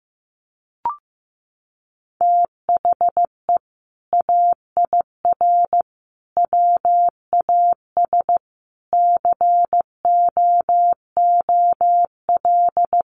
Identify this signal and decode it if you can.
Morse code, heard as a keyed tone, about 15 wpm, 700 Hz. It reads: THE AIR WAS COOL
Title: Sentences from Top 1200 Words 15wpm